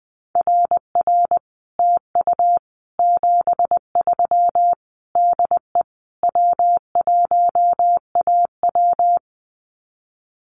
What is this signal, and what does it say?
Morse code, heard as a keyed tone, about 20 wpm, 700 Hz. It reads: RR TU 73 DE W1AW
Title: Common QSO Phrases Encore 20wpm